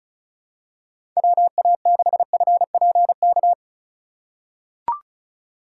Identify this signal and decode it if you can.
Morse code, heard as a keyed tone, about 36 wpm, 700 Hz. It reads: WA6FPK
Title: USA Call Signs - 6 Characters - 4x Character Spacing - 35wpm